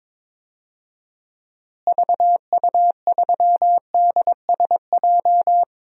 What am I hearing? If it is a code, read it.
Morse code, heard as a keyed tone, about 22 wpm, 700 Hz. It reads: VU3DSJ
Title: DXCC-100 Call Signs - 6 Characters - Repeated 3x - 22wpm